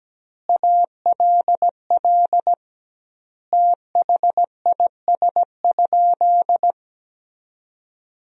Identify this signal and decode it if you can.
Morse code, heard as a keyed tone, about 17 wpm, 700 Hz. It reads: ALL THIS?
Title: 2x Word-Spacing - Sentences from Top 100 Words - 17wpm